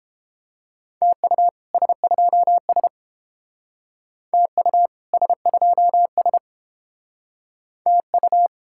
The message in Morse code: TU S2S TU S2S TU

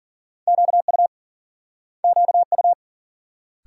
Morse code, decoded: QU QU

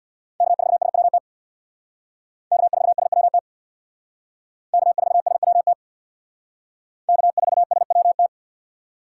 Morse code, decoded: K4SWT K4SWT K4SWT K4SWT

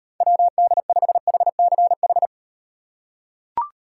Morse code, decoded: WD5HCH

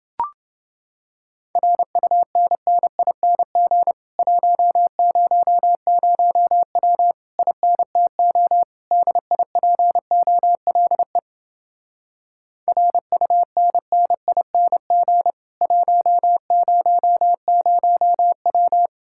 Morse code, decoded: RUNNING 100W INTO DIPOLE RUNNING 100W

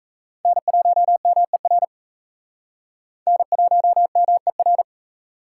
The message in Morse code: N1MER N1MER

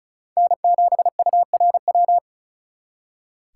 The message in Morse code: N7URW

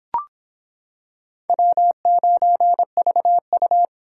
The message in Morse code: W9VU

